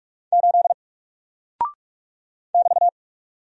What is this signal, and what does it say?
Morse code, heard as a keyed tone, about 48 wpm, 700 Hz. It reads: 8 <BT>
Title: Single Character (Letter:Number:Punctuation:Prosign) Alt 45wpm